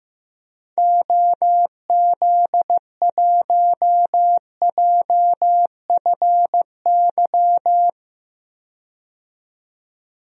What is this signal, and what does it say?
Morse code, heard as a keyed tone, about 15 wpm, 700 Hz. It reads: OZ1JFY
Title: DXCC-100 Call Signs - 6 Characters - 15wpm